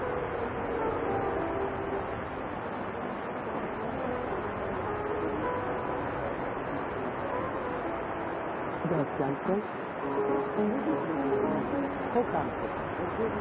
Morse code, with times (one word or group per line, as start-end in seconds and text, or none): none